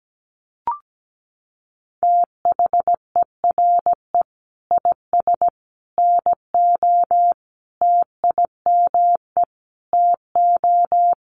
Morse code, heard as a keyed tone, about 17 wpm, 700 Hz. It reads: THERE IS NO TIME TO
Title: Sentences from Top 1100 Words 17wpm